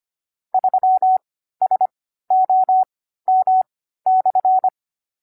Morse code, heard as a keyed tone, about 25 wpm, spaced as for 14 wpm, 750 Hz. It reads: 3SOM/
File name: CW-Koch / koch.29.3.groups5.15wpm